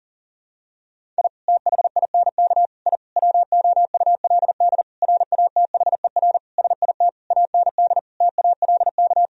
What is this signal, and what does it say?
Morse code, heard as a keyed tone, about 40 wpm, 700 Hz. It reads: I THINK I WOULD RATHER SIT AND TALK
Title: Sentences from Top 1000 Words - Review Part 4 - 40wpm